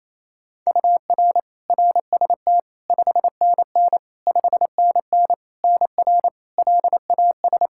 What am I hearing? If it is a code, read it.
Morse code, heard as a keyed tone, about 28 wpm, 700 Hz. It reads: UR RST 5NN 5NN NR LAS
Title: K1USN SST QSO Practice 28wpm